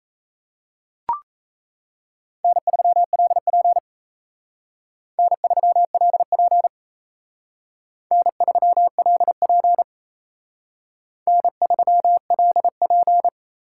Morse code, heard as a keyed tone, about 32 wpm, 700 Hz. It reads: N3LP N3LP N3LP N3LP
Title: Speed Racing - USA Call Signs 28wpm